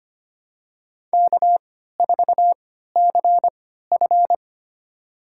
Morse code, heard as a keyed tone, about 25 wpm, spaced as for 14 wpm, 700 Hz. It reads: K4CF
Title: USA Call Signs - 4 Characters - 3x Character Spacing - 25wpm